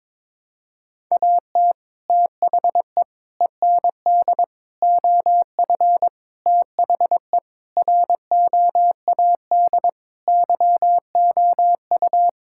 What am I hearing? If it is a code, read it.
Morse code, heard as a keyed tone, about 22 wpm, 700 Hz. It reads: AT THE END OF THE ROAD YOU
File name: practice / 22wpm / Sentences with 3-Word Phrases - 22wpm